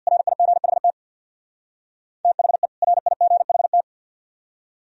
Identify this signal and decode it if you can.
Morse code, heard as a keyed tone, about 60 wpm, 700 Hz. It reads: RIGHT THE RIGHT
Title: Speed-Racing - Sets of 2 Words 50wpm